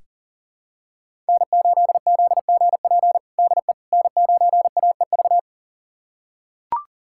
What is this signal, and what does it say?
Morse code, heard as a keyed tone, about 40 wpm, 700 Hz. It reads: N8ZGP DE N9AEV
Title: USA Callsign DE Callsign - 5 Characters - 40wpm